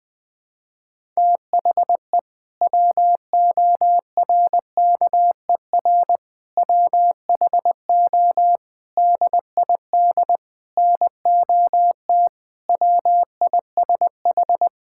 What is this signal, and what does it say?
Morse code, heard as a keyed tone, about 20 wpm, 700 Hz. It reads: THE WORKER WHO DID NOT WISH
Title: Sentences with 7-Word Phrases - 20wpm